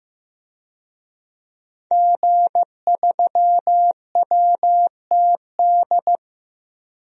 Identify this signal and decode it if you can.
Morse code, heard as a keyed tone, about 15 wpm, 700 Hz. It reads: G3WTD
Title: DXCC-100 Call Signs - 5 Characters - 15wpm